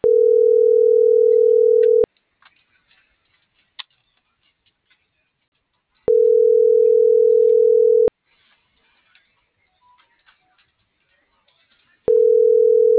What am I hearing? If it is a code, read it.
Ringback tone (North America), 440 + 480 Hz.